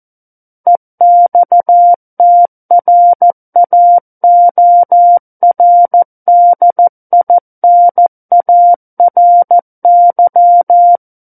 Morse code, heard as a keyed tone, about 14 wpm, 700 Hz. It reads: EXTRAORDINARY
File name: Book One / 05-Chapter 5 - 14 WPM